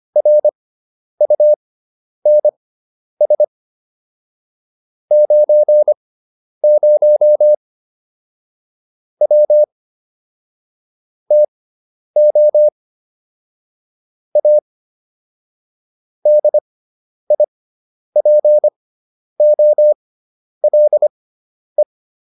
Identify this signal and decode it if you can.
Morse code, heard as a keyed tone, about 25 wpm, spaced as for 10 wpm, 600 Hz. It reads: RUNS 90 W TO A DIPOLE
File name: QSO304 10